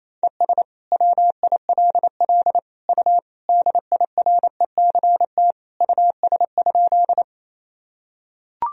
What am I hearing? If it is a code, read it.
Morse code, heard as a keyed tone, about 28 wpm, 700 Hz. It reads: ES WILL U DIRECT US?
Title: Sentences from Top 400 Words and Abbreviations 28wpm